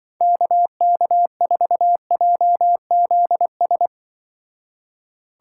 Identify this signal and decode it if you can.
Morse code, heard as a keyed tone, about 25 wpm, 700 Hz. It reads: KK4JZS
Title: Speed Racing - USA 2x3 Call Signs 20wpm